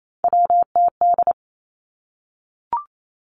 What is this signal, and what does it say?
Morse code, heard as a keyed tone, about 28 wpm, 700 Hz. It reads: WTD
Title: Three Challenging Letter-Number Combinations 28wpm